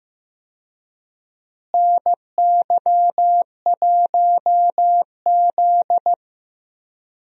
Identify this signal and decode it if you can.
Morse code, heard as a keyed tone, about 16 wpm, 700 Hz. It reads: NY1Z
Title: USA Call Signs - 4 Characters - Repeated 3x - 15wpm